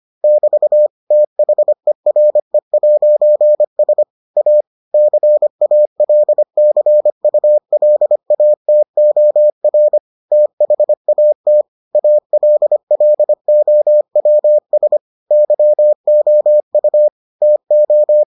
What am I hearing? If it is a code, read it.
Morse code, heard as a keyed tone, about 25 wpm, 600 Hz. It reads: <BT> THERE'S A CALCULATOR THAT ALLOWS YOU TO